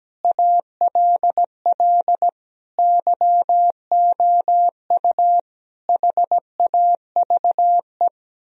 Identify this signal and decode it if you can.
Morse code, heard as a keyed tone, about 17 wpm, 700 Hz. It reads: ALL YOU HAVE
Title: Sentences from Top 1000 Words - Review Part 2 - 17wpm